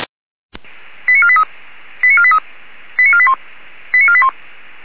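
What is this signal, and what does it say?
Five-tone selective call (ZVEI), heard as quick runs of five tones, tones of about 70 ms. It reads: 89483 89483 89481 89481
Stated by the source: Five-tone ZVEI selective call 89062, received by SDR.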